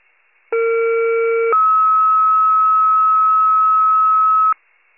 Two-tone page, heard as a one-second tone, then a three-second tone, about 460 then 1290 Hz. It is heard from 0.5 to 4.5 s.
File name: wcalert